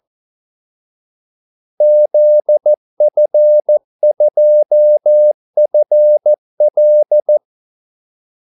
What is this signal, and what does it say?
Morse code, heard as a keyed tone, about 14 wpm, 600 Hz. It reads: ZF2FL
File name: Callsigns-14wpm